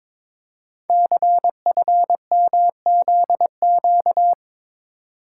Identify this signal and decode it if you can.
Morse code, heard as a keyed tone, about 22 wpm, 700 Hz. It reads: CFMZQ
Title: Code Groups - 5 Letters - 22wpm